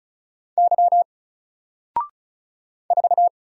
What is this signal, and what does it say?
Morse code, heard as a keyed tone, about 36 wpm, 700 Hz. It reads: Y 4
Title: Single Letter-Number 35wpm